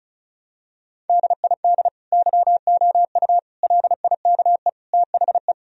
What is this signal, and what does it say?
Morse code, heard as a keyed tone, about 36 wpm, 700 Hz. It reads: DID YOU LIKE THE
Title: Sentences from Top 1000 Words - Review Part 4 - 35wpm